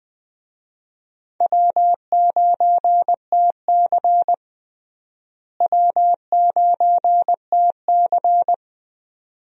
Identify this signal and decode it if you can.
Morse code, heard as a keyed tone, about 21 wpm, 700 Hz. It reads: W9TC W9TC
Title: USA Call Signs - 4 Characters - Repeated 2x - 20wpm